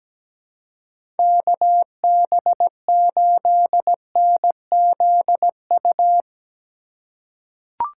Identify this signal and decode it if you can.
Morse code, heard as a keyed tone, about 17 wpm, 700 Hz. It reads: KB8NZU
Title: USA Call Signs - 2x Character Spacing - 17wpm